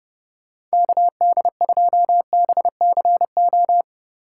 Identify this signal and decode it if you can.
Morse code, heard as a keyed tone, about 38 wpm, 700 Hz. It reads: KD2BCO